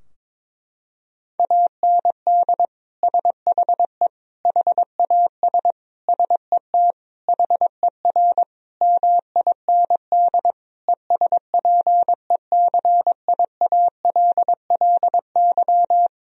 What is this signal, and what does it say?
Morse code, heard as a keyed tone, about 21 wpm, 700 Hz. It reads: AND SHE HAS SET HER MIND ESPECIALLY
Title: Sentences from Top 1000 Words - Review Part 4 - 22wpm